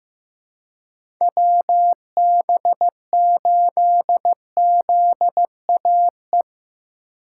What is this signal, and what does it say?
Morse code, heard as a keyed tone, about 15 wpm, 700 Hz. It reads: WB8ZAE
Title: USA Call Signs - 6 Characters - 3x Character Spacing - 15wpm